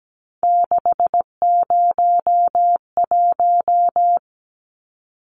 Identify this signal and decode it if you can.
Morse code, heard as a keyed tone, about 17 wpm, 700 Hz. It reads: 601
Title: Numbers - 3-Digit 17wpm